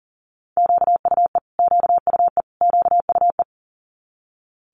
Morse code, heard as a keyed tone, about 40 wpm, 700 Hz. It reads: QUE QUE QUE
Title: Multi-letter Phonemes Repeated 3x 40wpm